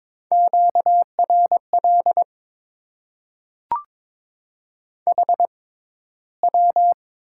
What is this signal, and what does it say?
Morse code, heard as a keyed tone, about 22 wpm, spaced as for 13 wpm, 700 Hz. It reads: QRL HW
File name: Top 90 QSO Elements - 6x Character Spacing - 22wpm